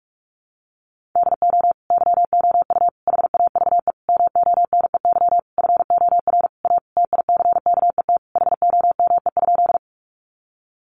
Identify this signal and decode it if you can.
Morse code, heard as a keyed tone, about 44 wpm, 700 Hz. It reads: DO YOU HAVE MONEY FOR A TICKET HOME?
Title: Sentences from Top 1400 Words 45wpm